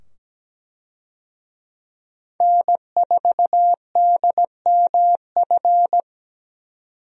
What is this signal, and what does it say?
Morse code, heard as a keyed tone, about 17 wpm, 700 Hz. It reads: N4DMF
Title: USA Call Signs - 5 Characters - 17wpm